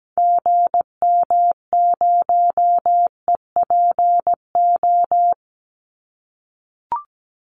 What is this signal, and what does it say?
Morse code, heard as a keyed tone, about 17 wpm, 700 Hz. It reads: GM0EPO
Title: Call Signs - 17wpm